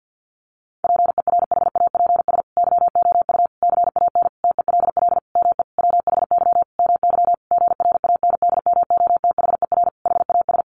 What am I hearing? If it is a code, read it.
Morse code, heard as a keyed tone, about 48 wpm, 700 Hz. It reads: PERHAPS YOU CAN TELL ME WHY MY GRANDMOTHER HAS
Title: Sentences from Top 1200 Words 50wpm